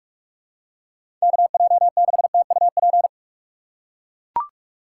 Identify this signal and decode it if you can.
Morse code, heard as a keyed tone, about 44 wpm, 700 Hz. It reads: KJ6TUP